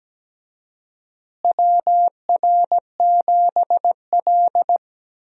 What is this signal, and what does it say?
Morse code, heard as a keyed tone, about 17 wpm, 700 Hz. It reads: WR7L
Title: USA Call Signs - 4 Characters - Repeated 3x - 17wpm